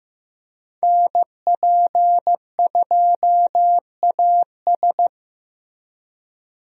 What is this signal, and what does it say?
Morse code, heard as a keyed tone, about 14 wpm, 700 Hz. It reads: NP2AS
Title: USA Call Signs - 5 Characters - Repeated 3x - 15wpm